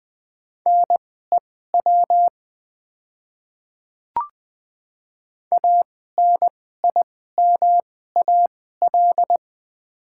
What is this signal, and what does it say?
Morse code, heard as a keyed tone, about 20 wpm, spaced as for 14 wpm, 700 Hz. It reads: NEW ANIMAL
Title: 2x Character Spacing - Top 200 Words - 20wpm